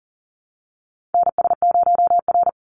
Morse code, heard as a keyed tone, about 40 wpm, 700 Hz. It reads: NS0R
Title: Call Signs Repeated 2x - 40wpm